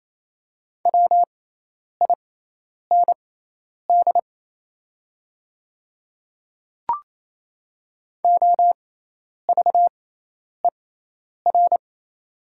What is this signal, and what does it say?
Morse code, heard as a keyed tone, about 28 wpm, spaced as for 10 wpm, 700 Hz. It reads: WIND OVER
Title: 6x Character Spacing - Top 500 Words - 28wpm